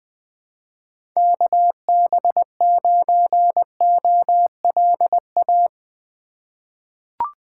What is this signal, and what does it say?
Morse code, heard as a keyed tone, about 20 wpm, 700 Hz. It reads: KB9OLA
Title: USA Call Signs - 6 Characters - 2x Character Spacing - 20wpm